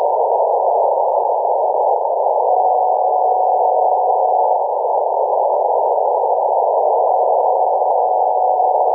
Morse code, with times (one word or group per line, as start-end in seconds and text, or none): none